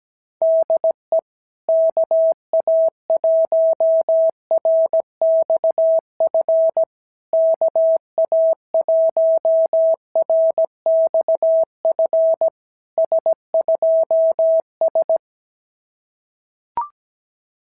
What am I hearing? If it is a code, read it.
Morse code, heard as a keyed tone, about 16 wpm, 650 Hz. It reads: DE KA1RXF KA1RXF S2S
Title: SOTA QSO Practice 17wpm